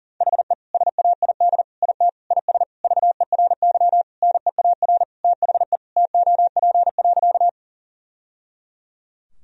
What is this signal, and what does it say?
Morse code, heard as a keyed tone, about 40 wpm, 700 Hz. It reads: HE SAID IT IS VERY NEAR THE TOP.
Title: Sentences from Top 1000 Words - Review Part 2 - 40wpm